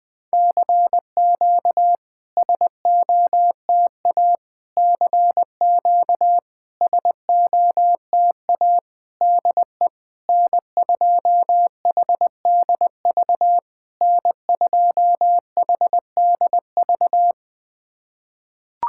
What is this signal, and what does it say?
Morse code, heard as a keyed tone, about 20 wpm, 700 Hz. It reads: CQ SOTA CQ SOTA DE N2HDV N2HDV
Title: SOTA QSO Practice 20wpm